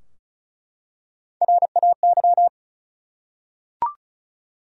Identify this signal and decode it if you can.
Morse code, heard as a keyed tone, about 36 wpm, 700 Hz. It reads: RAY
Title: USA Common Names - Max 4 Letters - 35wpm